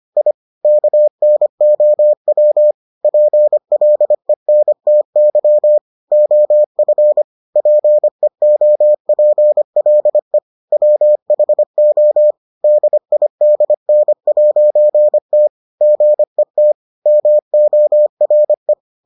Morse code, heard as a keyed tone, about 25 wpm, 600 Hz. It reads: I KNOW PLENTY OF PEOPLE WHO DIDN'T GET MORE